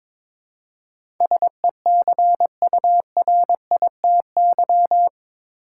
Morse code, heard as a keyed tone, about 21 wpm, 700 Hz. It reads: SECURITY
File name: practice / 22wpm / Eight Letter Words - 22wpm